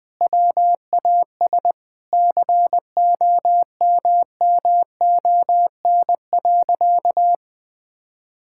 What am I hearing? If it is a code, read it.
Morse code, heard as a keyed tone, about 20 wpm, 700 Hz. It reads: WAS COMMON.
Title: Sentences with 6-Word Phrases - 20wpm